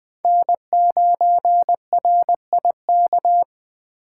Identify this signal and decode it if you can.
Morse code, heard as a keyed tone, about 21 wpm, 700 Hz. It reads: N9RIK
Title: USA Call Signs - 3x Character Spacing - 20wpm